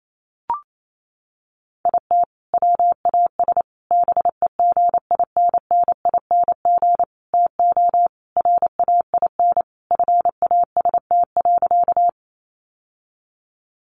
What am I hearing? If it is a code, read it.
Morse code, heard as a keyed tone, about 28 wpm, 700 Hz. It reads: IT WAS BEGINNING TO RAIN FAST.
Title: Sentences from Top 800 Words 28wpm